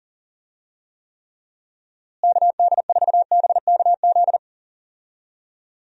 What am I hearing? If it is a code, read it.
Morse code, heard as a keyed tone, about 40 wpm, 700 Hz. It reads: KD4BKZ
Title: USA Call Signs - 6 Characters - Repeated 3x - 40wpm